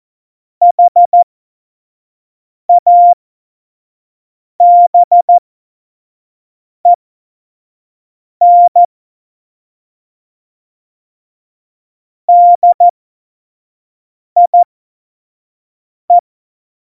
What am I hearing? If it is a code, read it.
Morse code, heard as a keyed tone, about 14 wpm, spaced as for 5 wpm, 700 Hz. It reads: HABEN DIE
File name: Morsekurs / Witze / Witz 11 Juli 17, Tempo 25 F70